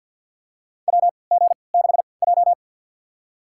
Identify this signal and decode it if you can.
Morse code, heard as a keyed tone, about 53 wpm, spaced as for 29 wpm, 700 Hz. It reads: WG6J
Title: USA Call Signs - 4 Characters - 3x Character Spacing - 50wpm